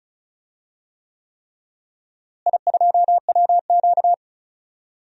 Morse code, heard as a keyed tone, about 36 wpm, 700 Hz. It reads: I2WQ
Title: DXCC-100 Call Signs - 4 Characters - Repeated 3x - 35wpm